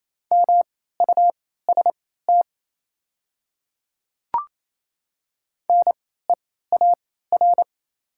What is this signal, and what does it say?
Morse code, heard as a keyed tone, about 28 wpm, spaced as for 16 wpm, 700 Hz. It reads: MUST NEAR
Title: 3x Character Spacing - Top 200 Words - 28wpm